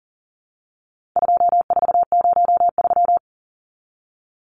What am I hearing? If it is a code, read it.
Morse code, heard as a keyed tone, about 40 wpm, 700 Hz. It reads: 2403
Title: Numbers - 4-Digit 40wpm